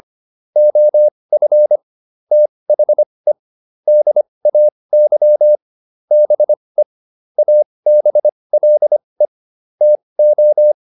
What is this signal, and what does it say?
Morse code, heard as a keyed tone, about 25 wpm, spaced as for 20 wpm, 600 Hz. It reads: OF THE DAY BE ABLE TO